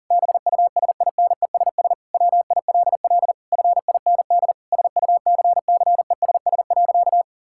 Morse code, heard as a keyed tone, about 40 wpm, 700 Hz. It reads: BUSINESS WILL FIND SUCCESS.
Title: Sentences from Top 1000 Words - Review Part 5 - 40wpm